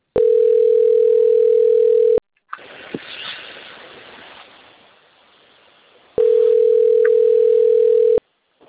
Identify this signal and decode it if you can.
Ringback tone (North America), 440 + 480 Hz.